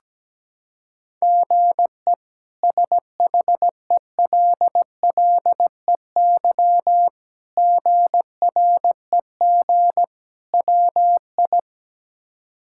Morse code, heard as a keyed tone, about 17 wpm, 700 Hz. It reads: GE SHELLEY GREG WI
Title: K1USN SST QSO Practice - 17wpm